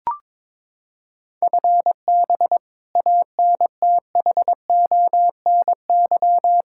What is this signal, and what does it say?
Morse code, heard as a keyed tone, about 22 wpm, 700 Hz. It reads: FB ANTHONY
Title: Practice QSO - Exchange 3 - More Info - 22wpm